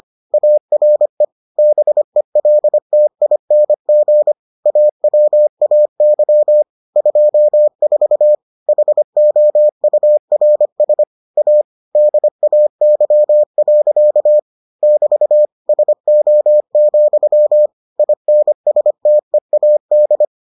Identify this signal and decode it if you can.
Morse code, heard as a keyed tone, about 25 wpm, 600 Hz. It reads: ARE BELTING AWAY 24 HOURS A DAY. <BT> SO, INSTEAD